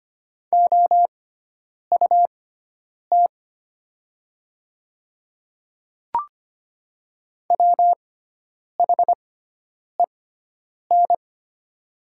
Morse code, heard as a keyed tone, about 25 wpm, spaced as for 9 wpm, 690 Hz. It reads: OUT WHEN
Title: 6x Character Spacing - Top 200 Words - 25wpm